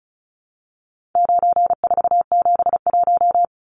Morse code, heard as a keyed tone, about 34 wpm, 700 Hz. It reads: 9471
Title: Numbers - 4-Digit 35wpm